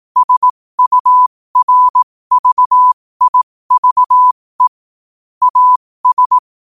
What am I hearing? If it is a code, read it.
Morse code, heard as a keyed tone, about 18 wpm, spaced as for 15 wpm, 1000 Hz. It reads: SURVIVE AS
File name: qotdfiles / 15wpm